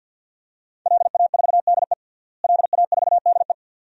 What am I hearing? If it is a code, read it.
Morse code, heard as a keyed tone, about 50 wpm, 700 Hz. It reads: LA4DE LA4DE
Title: DXCC-100 Call Signs - 5 Characters - Repeated 2x - 50wpm